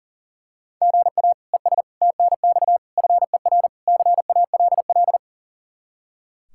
Morse code, heard as a keyed tone, about 40 wpm, 700 Hz. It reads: GA ES TNX FER CALL
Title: Practice QSO - Exchange 1 - Beginning - 40wpm